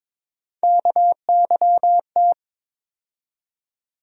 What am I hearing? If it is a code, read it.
Morse code, heard as a keyed tone, about 22 wpm, 700 Hz. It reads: KYT
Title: USA Call Sign Postfixes - 3 Letters 22wpm